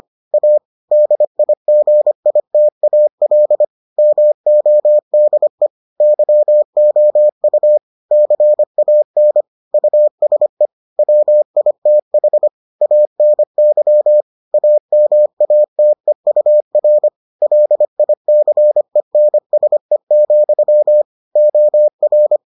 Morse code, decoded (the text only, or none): A DIGITAL MODE YOU CAN USE WITH ANY AMATEUR LICENSE, OR